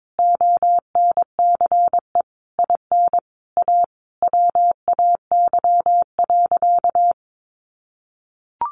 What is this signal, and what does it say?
Morse code, heard as a keyed tone, about 22 wpm, 700 Hz. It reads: ONCE IN A WAY.
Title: Sentences from Top 500 Words 22wpm